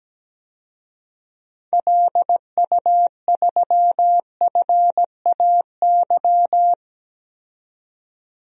Morse code, heard as a keyed tone, about 17 wpm, 700 Hz. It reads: LU3FAY
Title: DXCC-100 Call Signs - 6 Characters - 17wpm